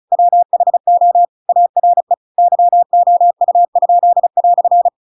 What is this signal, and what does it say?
Morse code, heard as a keyed tone, about 34 wpm, 700 Hz. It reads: WHO ARE YOU?"